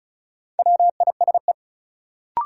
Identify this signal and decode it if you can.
Morse code, heard as a keyed tone, about 36 wpm, 700 Hz. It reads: WISE
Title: Sentences from Top 1600 Words - New Word Drill - 35wpm